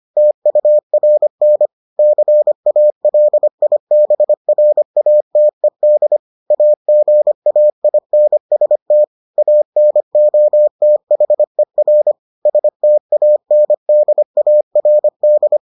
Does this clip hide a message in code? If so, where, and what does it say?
Yes: Morse code from 0.2 to 15.6 s, heard as a keyed tone, about 25 wpm, 600 Hz. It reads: TURN CALIBRATED AGAINST ANOTHER STANDARD